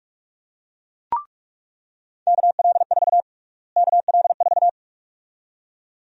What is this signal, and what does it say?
Morse code, heard as a keyed tone, about 46 wpm, 700 Hz. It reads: KL4 KL4
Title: USA Call Sign Prefixes - Repeated 2x - 45wpm